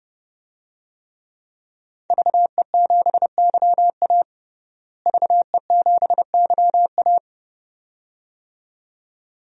Morse code, heard as a keyed tone, about 31 wpm, 700 Hz. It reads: VE7YA VE7YA
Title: DXCC-100 Call Signs - 5 Characters - Repeated 2x - 30wpm